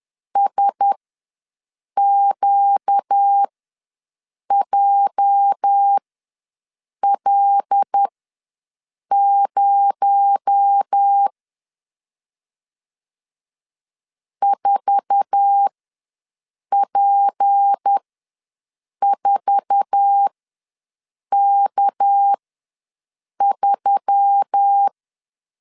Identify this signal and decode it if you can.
Morse code, heard as a keyed tone, about 10 wpm, spaced as for 6 wpm, 800 Hz. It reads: SQJL0 4P4K3
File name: LEZIONI / 8 YJ45 / 8 YJ4 ws 10wpm 4020